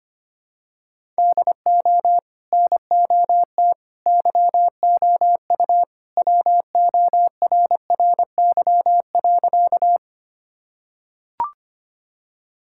Morse code, heard as a keyed tone, about 25 wpm, 700 Hz. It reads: DO NOT YOU WORRY.
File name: practice / 25wpm / Sentences with 3-Word Phrases - 25wpm